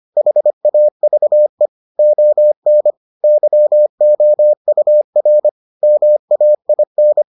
Morse code, heard as a keyed tone, about 25 wpm, 600 Hz. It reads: HAVE ON YOUR MAIN